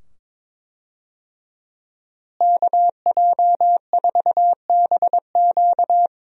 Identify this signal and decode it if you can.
Morse code, heard as a keyed tone, about 21 wpm, 700 Hz. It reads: KJ4BQ
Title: USA Call Signs - 5 Characters - Repeated 2x - 22wpm